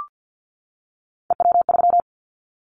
Morse code, heard as a keyed tone, about 50 wpm, 700 Hz. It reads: EW3
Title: Callsign Prefixes for Top 50 DXCC - 50wpm